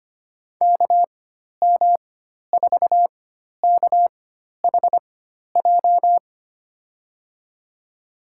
Morse code, heard as a keyed tone, about 25 wpm, spaced as for 12 wpm, 700 Hz. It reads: KM4KHJ
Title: USA Call Signs - 6 Characters - 4x Character Spacing - 25wpm